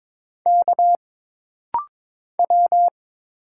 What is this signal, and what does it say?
Morse code, heard as a keyed tone, about 22 wpm, 700 Hz. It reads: K W
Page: This rapid-fire timing is geared to push your ICR (Instant Character Recognition) skills.